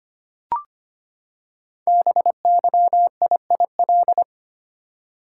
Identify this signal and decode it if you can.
Morse code, heard as a keyed tone, about 25 wpm, 700 Hz. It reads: BYIIL
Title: Code Groups - 5 Letters - 25wpm